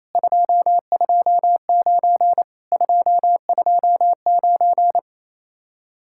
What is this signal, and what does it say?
Morse code, heard as a keyed tone, about 35 wpm, 700 Hz. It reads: 229 229